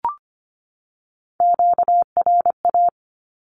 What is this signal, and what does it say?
Morse code, heard as a keyed tone, about 25 wpm, 700 Hz. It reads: QRA
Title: Q-Codes 25wpm